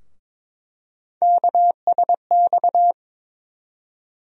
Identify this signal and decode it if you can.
Morse code, heard as a keyed tone, about 21 wpm, 700 Hz. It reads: KSX
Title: USA Call Sign Postfixes - 3 Letters 22wpm